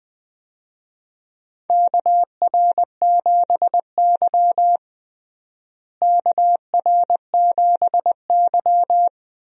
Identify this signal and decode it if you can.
Morse code, heard as a keyed tone, about 20 wpm, 700 Hz. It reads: KR7Y KR7Y
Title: USA Call Signs - 4 Characters - Repeated 3x - 20wpm